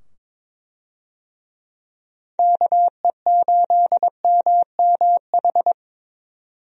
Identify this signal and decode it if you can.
Morse code, heard as a keyed tone, about 22 wpm, 700 Hz. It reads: KE8MMH
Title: USA Call Signs - 6 Characters - 22wpm